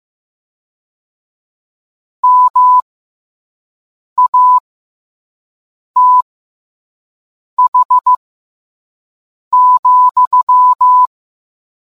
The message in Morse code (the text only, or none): MATH,